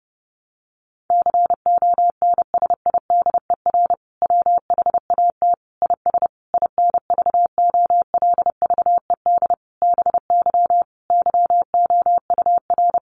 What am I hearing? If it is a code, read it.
Morse code, heard as a keyed tone, about 30 wpm, 700 Hz. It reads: CONSIDER WHAT IS INVOLVED BY YOUR
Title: Sentences from Top 800 Words 30wpm